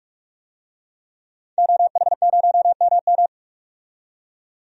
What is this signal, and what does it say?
Morse code, heard as a keyed tone, about 46 wpm, 700 Hz. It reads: OH0MM
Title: DXCC-100 Call Signs - 5 Characters - 45wpm